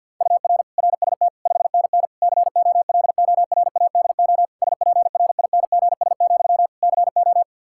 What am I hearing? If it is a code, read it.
Morse code, heard as a keyed tone, about 51 wpm, 700 Hz. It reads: UR RST 5NN COLORADO SPRINGS, CO
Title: Practice QSO - Exchange 1 - Beginning - 50wpm